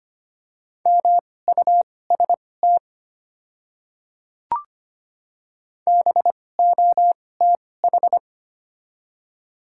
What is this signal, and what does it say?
Morse code, heard as a keyed tone, about 25 wpm, spaced as for 18 wpm, 700 Hz. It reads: MUST BOTH
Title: 2x Character Spacing - Top 300 Words - 25wpm